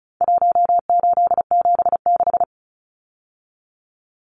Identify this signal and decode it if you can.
Morse code, heard as a keyed tone, about 34 wpm, 700 Hz. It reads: 1876
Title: Numbers - Contest Style - 4-Digit 35wpm